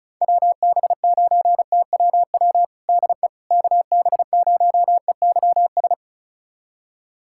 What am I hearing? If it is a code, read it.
Morse code, heard as a keyed tone, about 34 wpm, 700 Hz. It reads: WB9TWW DE KB0EYS
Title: USA Callsign DE Callsign - 6 Characters - 35wpm